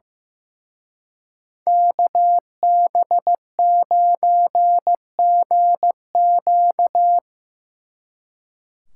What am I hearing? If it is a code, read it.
Morse code, heard as a keyed tone, about 17 wpm, 700 Hz. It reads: KB9GQ
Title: Speed Racing - USA 2x2 Call Signs 15wpm